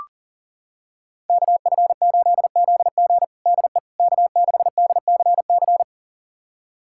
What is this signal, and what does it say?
Morse code, heard as a keyed tone, about 40 wpm, 700 Hz. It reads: KF8ZG DE K6DCC
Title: USA Callsign DE Callsign - 5 Characters - 40wpm